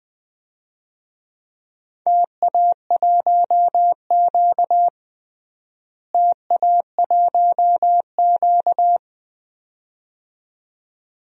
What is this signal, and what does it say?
Morse code, heard as a keyed tone, about 20 wpm, 700 Hz. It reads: TA1Q TA1Q
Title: DXCC-100 Call Signs - 4 Characters - Repeated 2x - 20wpm